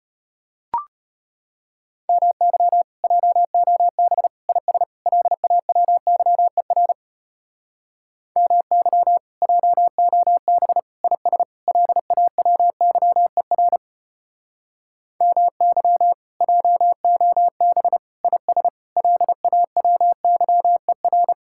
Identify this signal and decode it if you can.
Morse code, heard as a keyed tone, about 30 wpm, 700 Hz. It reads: MY JOB IS LAWYER MY JOB IS LAWYER MY JOB IS LAWYER
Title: Speed-Racing Common QSO Phrases - 25wpm